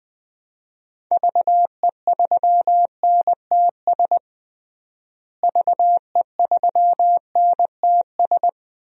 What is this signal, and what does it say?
Morse code, heard as a keyed tone, about 20 wpm, 700 Hz. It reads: VE3NTS VE3NTS
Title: DXCC-100 Call Signs - 6 Characters - Repeated 3x - 20wpm